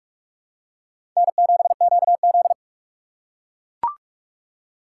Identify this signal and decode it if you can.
Morse code, heard as a keyed tone, about 42 wpm, 700 Hz. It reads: N7QZ